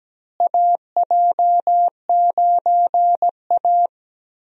Morse code, heard as a keyed tone, about 17 wpm, 700 Hz. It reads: AJ9A
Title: USA Call Signs - 4 Characters - Repeated 3x - 17wpm